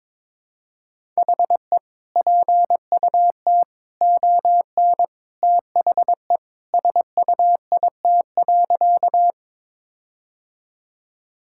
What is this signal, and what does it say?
Morse code, heard as a keyed tone, about 22 wpm, 700 Hz. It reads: HE PUT ON THE SUIT.
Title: Sentences from Top 1600 Words 22wpm